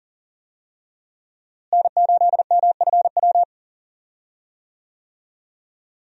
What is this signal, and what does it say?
Morse code, heard as a keyed tone, about 40 wpm, 700 Hz. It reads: N8MFW
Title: USA Call Signs - 5 Characters - Repeated 3x - 40wpm